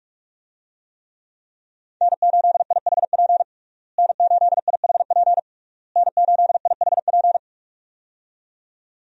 Morse code, heard as a keyed tone, about 41 wpm, 700 Hz. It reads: N8IHP N8IHP N8IHP
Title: USA Call Signs - 5 Characters - Repeated 3x - 45wpm